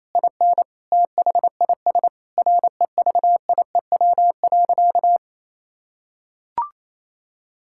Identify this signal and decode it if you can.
Morse code, heard as a keyed tone, about 28 wpm, 700 Hz. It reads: IN THIS REVIEW.
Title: Sentences from Top 1000 Words - Review Part 6 - 28wpm